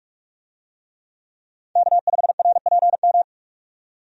Morse code, heard as a keyed tone, about 44 wpm, 700 Hz. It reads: K5RPM
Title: USA Call Signs - 5 Characters - 45wpm